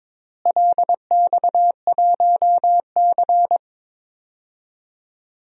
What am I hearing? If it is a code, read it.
Morse code, heard as a keyed tone, about 22 wpm, 700 Hz. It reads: LX1C
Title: DXCC-100 Call Signs - 4 Characters - Repeated 2x - 22wpm